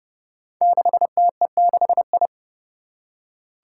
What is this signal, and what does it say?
Morse code, heard as a keyed tone, about 30 wpm, 700 Hz. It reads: 6TE6I
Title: Five Challenging Letter-Number Combinations 30wpm